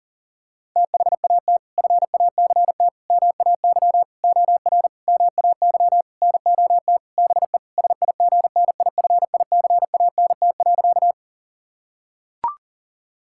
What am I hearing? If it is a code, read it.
Morse code, heard as a keyed tone, about 40 wpm, 700 Hz. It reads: THAT FACT MAY OR MAY NOT BE SIGNIFICANT.
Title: Sentences from Top 1000 Words - Review Part 6 - 40wpm